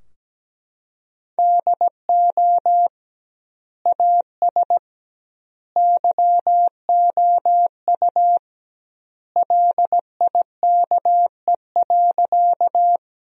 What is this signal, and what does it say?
Morse code, heard as a keyed tone, about 17 wpm, 700 Hz. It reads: DO AS YOU LIKE.
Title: 2x Word-Spacing - Sentences from Top 100 Words - 17wpm